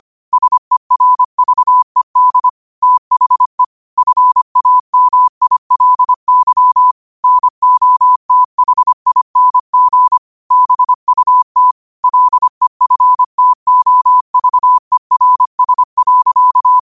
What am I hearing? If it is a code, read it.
Morse code, heard as a keyed tone, about 25 wpm, 1000 Hz. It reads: SERVED THE FAMILY NOTHING BUT LEFTOVERS.
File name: qotdfiles / 25wpm